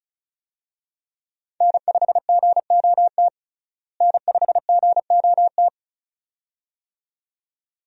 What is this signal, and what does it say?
Morse code, heard as a keyed tone, about 36 wpm, 700 Hz. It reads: N5GOT N5GOT
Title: USA Call Signs - 5 Characters - Repeated 2x - 35wpm